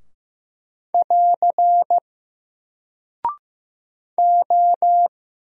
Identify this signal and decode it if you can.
Morse code, heard as a keyed tone, about 15 wpm, 700 Hz. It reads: <AR> O
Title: Single Character (Letter:Number:Punctuation:Prosign) Alt 15wpm